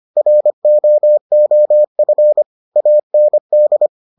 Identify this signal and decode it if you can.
Morse code, heard as a keyed tone, about 25 wpm, 600 Hz. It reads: ROOF AND